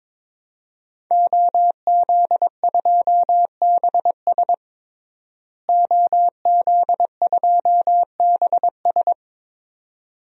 Morse code, heard as a keyed tone, about 21 wpm, 700 Hz. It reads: OZ2BS OZ2BS
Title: DXCC-100 Call Signs - 5 Characters - Repeated 2x - 22wpm